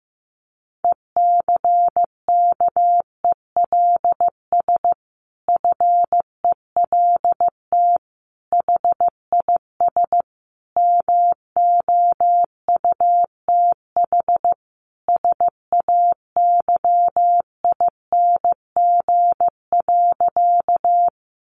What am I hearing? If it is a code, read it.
Morse code, heard as a keyed tone, about 15 wpm, 700 Hz. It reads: ECKELS FELT HIS MOUTH SAYING.
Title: A Sound Of Thunder 15wpm